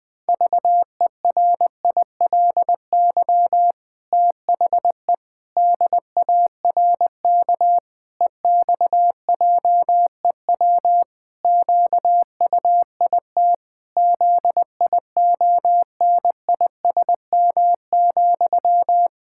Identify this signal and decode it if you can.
Morse code, heard as a keyed tone, about 20 wpm, 700 Hz. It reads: VERILY THE DARK EXJEW QUIT ZIONISM,
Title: Pangram Challenge v2 - 20wpm